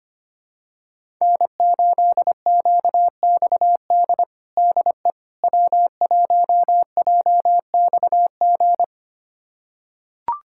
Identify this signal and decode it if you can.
Morse code, heard as a keyed tone, about 25 wpm, 700 Hz. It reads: N8QXD DE W1JXG
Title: USA Callsign DE Callsign 25wpm